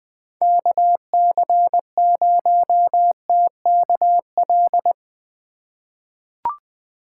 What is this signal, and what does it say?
Morse code, heard as a keyed tone, about 20 wpm, 700 Hz. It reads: KC0TKL
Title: USA Call Signs - 6 Characters - 2x Character Spacing - 20wpm